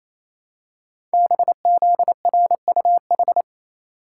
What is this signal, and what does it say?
Morse code, heard as a keyed tone, about 28 wpm, 700 Hz. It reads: BZRUH
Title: Code Groups - 5 Letters - 28wpm